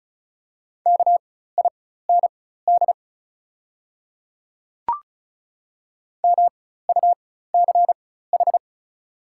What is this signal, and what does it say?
Morse code, heard as a keyed tone, about 36 wpm, spaced as for 16 wpm, 700 Hz. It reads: KIND MUCH
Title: 4x Character Spacing - Top 200 Words - 35wpm